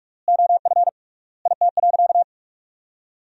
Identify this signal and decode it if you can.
Morse code, heard as a keyed tone, about 46 wpm, 700 Hz. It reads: OF IT.
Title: 3x Word-Spacing - Sentences from Top 200 Words - 45wpm